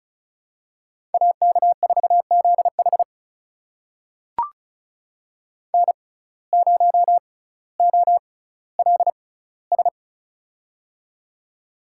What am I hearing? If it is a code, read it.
Morse code, heard as a keyed tone, about 36 wpm, spaced as for 18 wpm, 700 Hz. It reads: AK4ZH N0OLS
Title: USA Call Signs - 5 Characters - 6x Character Spacing - 35wpm